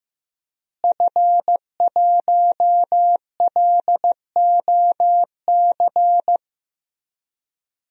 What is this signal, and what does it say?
Morse code, heard as a keyed tone, about 15 wpm, 700 Hz. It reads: F1LOC
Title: DXCC-100 Call Signs - 5 Characters - Repeated 3x - 15wpm